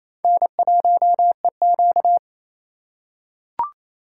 Morse code, heard as a keyed tone, about 28 wpm, 700 Hz. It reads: N1EQ
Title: USA Call Signs - 4 Characters - 4x Character Spacing - 28wpm